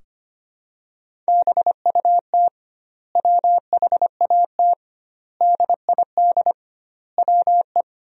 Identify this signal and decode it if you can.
Morse code, heard as a keyed tone, about 25 wpm, 700 Hz. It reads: BUT WHAT DID WE
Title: 2x Word-Spacing - Sentences from Top 400 Words - 25wpm